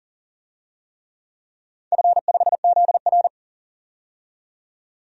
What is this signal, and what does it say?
Morse code, heard as a keyed tone, about 40 wpm, 700 Hz. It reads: F5ZR
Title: DXCC-100 Call Signs - 4 Characters - 40wpm